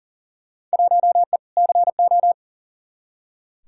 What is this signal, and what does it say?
Morse code, heard as a keyed tone, about 40 wpm, 700 Hz. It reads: 1E CO
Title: Field Day Exchanges - 40wpm